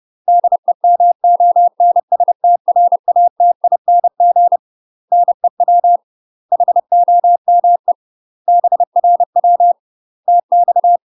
Morse code, heard as a keyed tone, about 30 wpm, 700 Hz. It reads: DEMONSTRATING NEW HOME BRW TX
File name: UK MASTER QSOs TRAINING / 30WPM / UK MASTER QSOs 0017